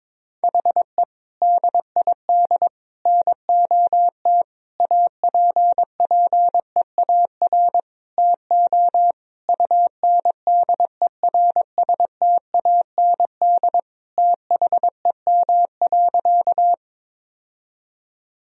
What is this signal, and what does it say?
Morse code, heard as a keyed tone, about 22 wpm, 700 Hz. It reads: HE DID NOT APPEAR TO UNDERSTAND THEM.
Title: Sentences from Top 1000 Words - Review Part 2 - 22wpm